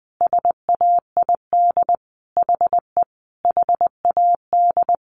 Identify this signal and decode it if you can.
Morse code, heard as a keyed tone, about 20 wpm, 700 Hz. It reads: SAID HE HAD
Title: Sentences from Top 500 Words 20wpm